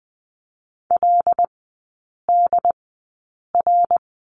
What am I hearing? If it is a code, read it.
Morse code, heard as a keyed tone, about 20 wpm, spaced as for 8 wpm, 700 Hz. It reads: LDR
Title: 100 Lesson 17e - Letter u - Three Characters 20wpm